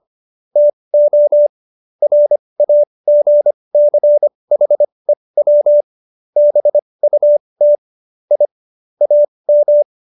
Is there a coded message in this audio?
Yes: Morse code from 0.6 to 9.8 s, heard as a keyed tone, about 25 wpm, spaced as for 20 wpm, 590 Hz. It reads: TO RAGCHEW BUT I AM